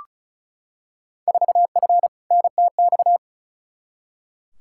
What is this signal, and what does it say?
Morse code, heard as a keyed tone, about 35 wpm, 700 Hz. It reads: 4F NTX
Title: Field Day Exchanges Encore - 35wpm